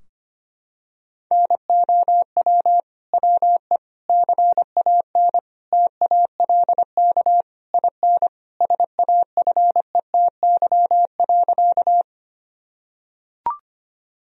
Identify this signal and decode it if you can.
Morse code, heard as a keyed tone, about 25 wpm, 700 Hz. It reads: NOW WE CAN TALK IN SAFETY.
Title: Sentences from Top 1000 Words - Review Part 4 - 25wpm